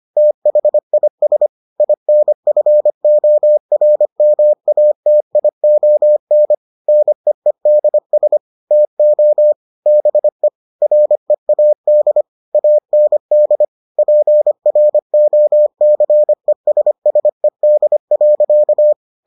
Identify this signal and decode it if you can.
Morse code, heard as a keyed tone, about 25 wpm, 600 Hz. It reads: THIS INFORMATION NEEDS TO BE READ AND PROCESSED.